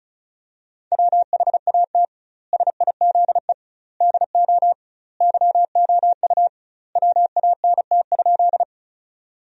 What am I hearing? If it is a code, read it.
Morse code, heard as a keyed tone, about 35 wpm, 700 Hz. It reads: WHAT SIZE DO YOU WANT?
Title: 2x Word-Spacing - Sentences from Top 400 Words - 35wpm